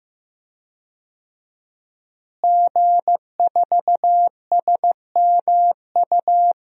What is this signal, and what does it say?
Morse code, heard as a keyed tone, about 15 wpm, 700 Hz. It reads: G4SMU
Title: DXCC-100 Call Signs - 5 Characters - Repeated 3x - 15wpm